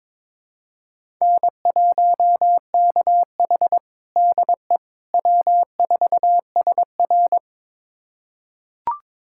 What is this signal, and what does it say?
Morse code, heard as a keyed tone, about 21 wpm, 700 Hz. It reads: N1KH DE W4SR
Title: USA Callsign DE Callsign - 4 Characters - 22wpm